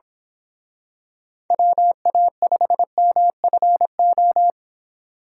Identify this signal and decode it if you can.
Morse code, heard as a keyed tone, about 26 wpm, 700 Hz. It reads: WA5MFO
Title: Speed Racing - USA 2x3 Call Signs 22wpm